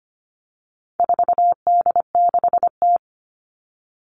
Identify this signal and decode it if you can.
Morse code, heard as a keyed tone, about 25 wpm, 700 Hz. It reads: 4D6T
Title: All Cut Numbers - 4-Digit 25wpm